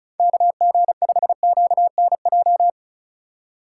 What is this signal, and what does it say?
Morse code, heard as a keyed tone, about 34 wpm, 700 Hz. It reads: KG5QNJ
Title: USA Call Signs - 6 Characters - 3x Character Spacing - 35wpm